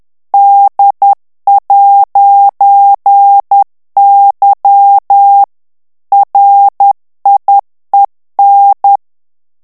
Morse code, heard as a keyed tone, about 10 wpm, 800 Hz. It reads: D'Y RIEN
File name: morse mp3 / FRANCAIS TEXTE CLAIR 2 VITESSE 10 MOTS MINUTES